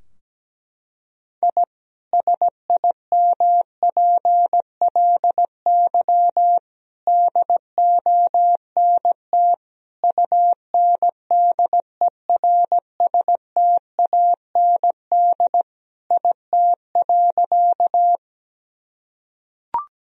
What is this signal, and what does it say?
Morse code, heard as a keyed tone, about 17 wpm, 700 Hz. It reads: I SIMPLY DONT UNDERSTAND IT.
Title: Sentences from Top 1000 Words - Review Part 4 - 17wpm